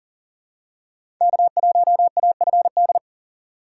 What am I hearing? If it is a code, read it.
Morse code, heard as a keyed tone, about 37 wpm, 690 Hz. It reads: K1AFD
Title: USA Call Signs - 5 Characters - 3x Character Spacing - 40wpm